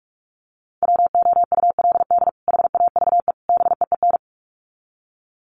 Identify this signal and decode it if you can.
Morse code, heard as a keyed tone, about 44 wpm, 700 Hz. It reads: WOULD HAVE BEEN
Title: Sets of 3 Words 45wpm